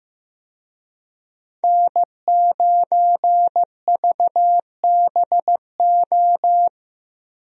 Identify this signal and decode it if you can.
Morse code, heard as a keyed tone, about 15 wpm, 700 Hz. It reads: N9VBO
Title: USA Call Signs - 5 Characters - Repeated 3x - 15wpm